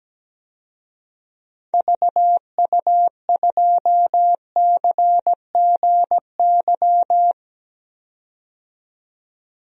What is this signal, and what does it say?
Morse code, heard as a keyed tone, about 17 wpm, 700 Hz. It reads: VU2CGY
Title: DXCC-100 Call Signs - 6 Characters - 17wpm